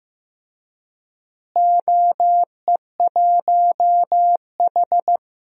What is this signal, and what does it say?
Morse code, heard as a keyed tone, about 15 wpm, 700 Hz. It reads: OE1H
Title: DXCC-100 Call Signs - 4 Characters - Repeated 3x - 15wpm